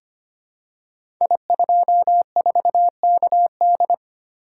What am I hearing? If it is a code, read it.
Morse code, heard as a keyed tone, about 25 wpm, 700 Hz. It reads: I24KD
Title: Code Groups - 5 Letter:Numbers - 25wpm